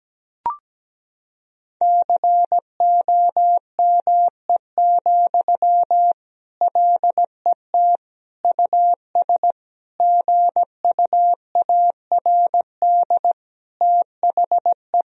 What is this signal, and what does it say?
Morse code, heard as a keyed tone, about 17 wpm, 700 Hz. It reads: COME, LET US GUARD THE
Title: Sentences from Top 1000 Words - Review Part 4 - 17wpm